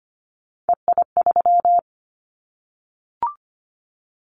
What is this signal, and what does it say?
Morse code, heard as a keyed tone, about 25 wpm, 700 Hz. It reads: EI3
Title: Callsign Prefixes for Top 50 DXCC - 25wpm